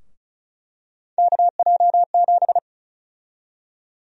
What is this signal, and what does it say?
Morse code, heard as a keyed tone, about 36 wpm, 700 Hz. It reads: KJ7
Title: USA Call Sign Prefixes - Repeated 2x - 35wpm